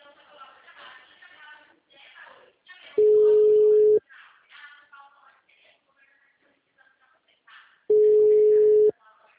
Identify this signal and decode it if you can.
Ringback tone (Europe, 425 Hz), with speech.